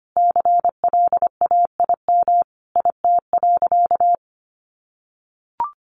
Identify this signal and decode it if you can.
Morse code, heard as a keyed tone, about 25 wpm, 700 Hz. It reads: CLAIM IT.
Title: Sentences from Top 1500 Words 25wpm